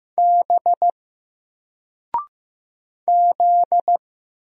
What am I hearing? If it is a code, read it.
Morse code, heard as a keyed tone, about 15 wpm, 700 Hz. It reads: B Z
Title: Single Character (Letter:Number:Punctuation:Prosign) Alt 15wpm